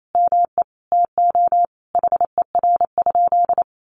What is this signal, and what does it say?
Morse code, heard as a keyed tone, about 28 wpm, 700 Hz. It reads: ME TO HER?
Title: Sentences from Top 200 Words 28wpm